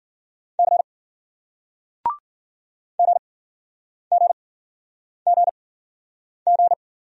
Morse code, heard as a keyed tone, about 51 wpm, spaced as for 9 wpm, 700 Hz. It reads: C GGGG